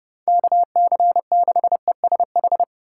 Morse code, heard as a keyed tone, about 29 wpm, 700 Hz. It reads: KC6ESH